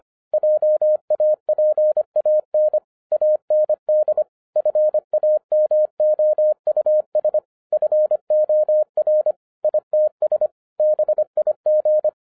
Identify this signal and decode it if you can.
Morse code, heard as a keyed tone, about 26 wpm, 610 Hz. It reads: JAPAN AND FAMOUS FOR ITS BIG
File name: 2011-10-mp3 / n04-JUST MOVED 25wpm